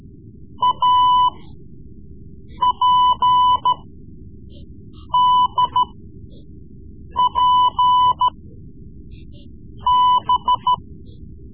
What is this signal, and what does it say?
Morse code, heard as a keyed tone, about 14 wpm, spaced as for 5 wpm, 1000 Hz. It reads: APDPB
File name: Morse Lessons / Morse Lessons / Tape-3B